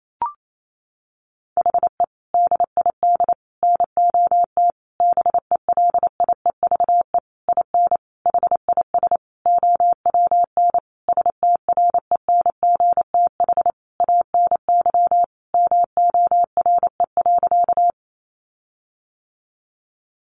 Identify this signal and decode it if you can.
Morse code, heard as a keyed tone, about 28 wpm, 700 Hz. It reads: HE DID NOT BELIEVE IN HIS OWN STRENGTH ANY MORE.
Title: Sentences from Top 1300 Words 28wpm